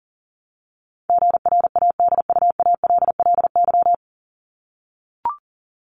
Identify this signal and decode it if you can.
Morse code, heard as a keyed tone, about 40 wpm, 700 Hz. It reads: GRADUALLY
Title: LY Suffix - Top 100 Words with Base Words 40wpm